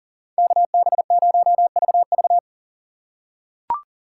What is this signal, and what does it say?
Morse code, heard as a keyed tone, about 40 wpm, 700 Hz. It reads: KB0VV
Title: USA Call Signs - 5 Characters - 4x Character Spacing - 40wpm